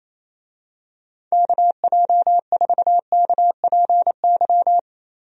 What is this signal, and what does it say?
Morse code, heard as a keyed tone, about 28 wpm, 700 Hz. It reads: KJ4KPY
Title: USA Call Signs - 6 Characters - Repeated 2x - 28wpm